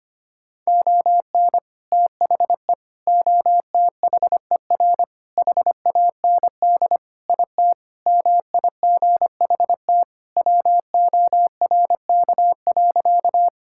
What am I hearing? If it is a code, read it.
Morse code, heard as a keyed tone, about 25 wpm, 700 Hz. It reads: ON THE OTHER HAND IT MIGHT WORK.
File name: practice / 25wpm / Sentences with 3-Word Phrases - 25wpm